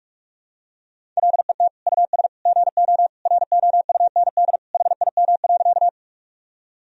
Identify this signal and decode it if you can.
Morse code, heard as a keyed tone, about 46 wpm, 700 Hz. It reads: LET US GO ROUND HIM.
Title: Sentences from Top 1000 Words - Review Part 1 - 45wpm